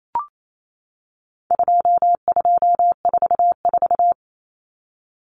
Morse code, heard as a keyed tone, about 28 wpm, 700 Hz. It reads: 2244
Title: Common Cut Numbers - 4-Digit 28wpm